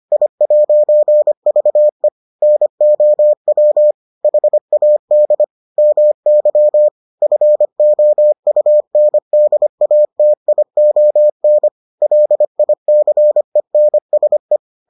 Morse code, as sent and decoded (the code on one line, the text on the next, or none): .. .----. ...- . / -. --- .-- / .... .- -.. / -- -.-- / ..-. --- ..- -. -.. .- - .. --- -. / .-.. .. -.-. -. ... .
I'VE NOW HAD MY FOUNDATION LICENSE